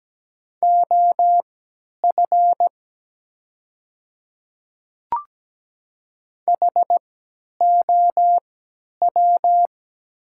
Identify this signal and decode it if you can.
Morse code, heard as a keyed tone, about 17 wpm, spaced as for 10 wpm, 700 Hz. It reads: OF HOW
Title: 3x Character Spacing - Top 100 Words - 17wpm